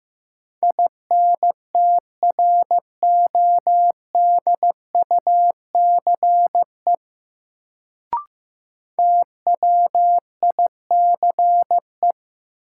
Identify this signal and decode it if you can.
Morse code, heard as a keyed tone, about 15 wpm, 700 Hz. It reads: INTRODUCE TWICE
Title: Sentences from Top 1600 Words - New Word Drill Alt - 15wpm